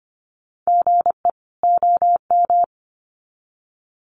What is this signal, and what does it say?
Morse code, heard as a keyed tone, about 25 wpm, 700 Hz. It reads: GE OM
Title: Common QSO Phrases 25wpm